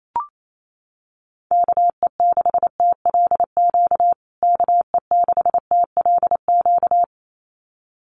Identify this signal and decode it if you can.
Morse code, heard as a keyed tone, about 28 wpm, 700 Hz. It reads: KE6TLQ KE6TLQ
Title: USA Call Signs Repeated 2x 28wpm